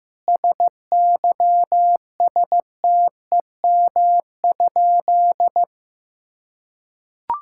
Morse code, heard as a keyed tone, about 15 wpm, 700 Hz. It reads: SYSTEM?
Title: Sentences from Top 1000 Words - Review Part 5 - 15wpm